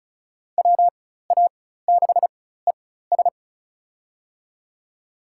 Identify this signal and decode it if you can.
Morse code, heard as a keyed tone, about 34 wpm, spaced as for 16 wpm, 700 Hz. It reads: WA6ES
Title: USA Call Signs - 5 Characters - 4x Character Spacing - 35wpm